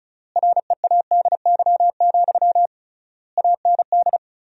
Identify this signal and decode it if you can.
Morse code, heard as a keyed tone, about 34 wpm, 700 Hz. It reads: READY, AND
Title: 3x Word-Spacing - Sentences from Top 500 Words - 35wpm